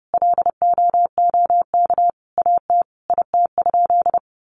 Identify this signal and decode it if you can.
Morse code, heard as a keyed tone, about 30 wpm, 700 Hz. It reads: LOOK AT IT?
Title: Sentences from Top 100 Words 30wpm